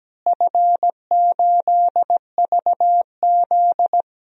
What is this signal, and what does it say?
Morse code, heard as a keyed tone, about 17 wpm, 700 Hz. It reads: F8VZ
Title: DXCC-100 Call Signs - 4 Characters - Repeated 3x - 17wpm